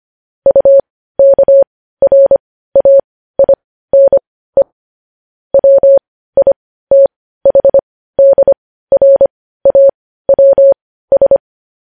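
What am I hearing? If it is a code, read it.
Morse code, heard as a keyed tone, about 25 wpm, spaced as for 15 wpm, 550 Hz. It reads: UKRAINE WITHDRAWS
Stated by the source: Morse code news headlines sent at 15, opening with vvv.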